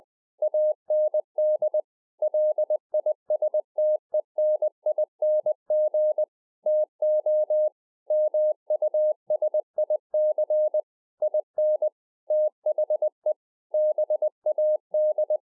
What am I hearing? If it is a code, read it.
Morse code, heard as a keyed tone, about 20 wpm, 610 Hz. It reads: AND LISTENING TO MUSIC IN THE BAD
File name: 20WPM